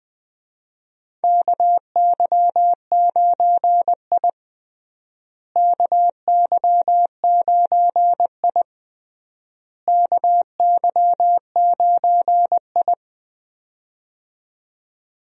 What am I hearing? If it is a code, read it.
Morse code, heard as a keyed tone, about 20 wpm, 700 Hz. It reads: KY9I KY9I KY9I
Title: USA Call Signs - 4 Characters - Repeated 3x - 20wpm